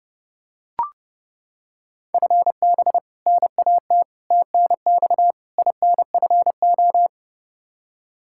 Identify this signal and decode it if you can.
Morse code, heard as a keyed tone, about 30 wpm, 700 Hz. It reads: FB NAT TNX INFO
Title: Practice QSO - Exchange 3 - More Info - 30wpm